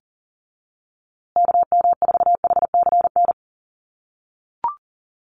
Morse code, heard as a keyed tone, about 40 wpm, 700 Hz. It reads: KM4HCN
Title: USA Call Signs Repeated 2x 40wpm